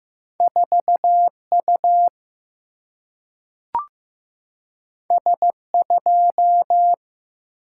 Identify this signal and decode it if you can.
Morse code, heard as a keyed tone, about 15 wpm, 700 Hz. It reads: 4 U S 2
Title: Single-Letter Single-Number 15wpm